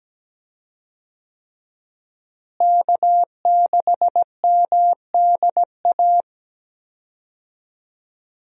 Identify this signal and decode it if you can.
Morse code, heard as a keyed tone, about 17 wpm, 700 Hz. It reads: K6MDA
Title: USA Call Signs - 5 Characters - Repeated 3x - 17wpm